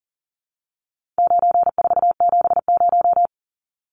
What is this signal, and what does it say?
Morse code, heard as a keyed tone, about 40 wpm, 700 Hz. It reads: 9470
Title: Numbers - 4-Digit 40wpm